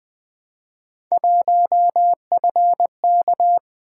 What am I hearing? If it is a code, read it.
Morse code, heard as a keyed tone, about 20 wpm, 700 Hz. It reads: 1FK
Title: Three Letter-Number Combinations - 20wpm